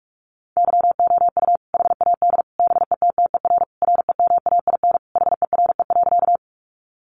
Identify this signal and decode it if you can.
Morse code, heard as a keyed tone, about 44 wpm, 700 Hz. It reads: YOU HAD BETTER REMAIN HERE.
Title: Sentences from Top 1200 Words 45wpm